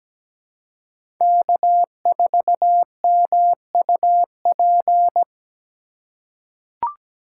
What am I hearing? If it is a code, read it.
Morse code, heard as a keyed tone, about 17 wpm, 700 Hz. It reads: K4MUP
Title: USA Call Signs - 5 Characters - 2x Character Spacing - 17wpm